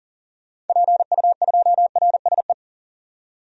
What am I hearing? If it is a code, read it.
Morse code, heard as a keyed tone, about 40 wpm, 700 Hz. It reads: PU2RSE
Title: DXCC-100 Call Signs - 6 Characters - 40wpm